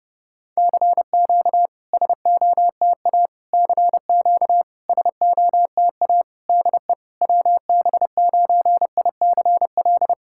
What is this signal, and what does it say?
Morse code, heard as a keyed tone, about 30 wpm, 700 Hz. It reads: CQ SOTA CQ SOTA DE WB9ICL
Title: SOTA QSO Practice 30wpm